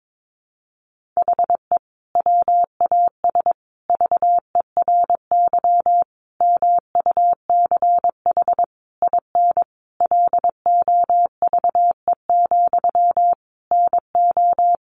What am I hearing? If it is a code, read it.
Morse code, heard as a keyed tone, about 22 wpm, 700 Hz. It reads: HE WAS VERY MUCH IN LOVE, NO
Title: Sentences from Top 1500 Words 22wpm